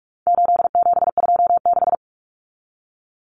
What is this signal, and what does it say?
Morse code, heard as a keyed tone, about 44 wpm, 700 Hz. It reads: 8726
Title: Common Cut Numbers - 4-Digit 45wpm